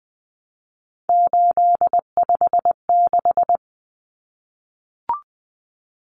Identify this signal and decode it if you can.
Morse code, heard as a keyed tone, about 20 wpm, 700 Hz. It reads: 856
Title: Numbers - 3-Digit 20wpm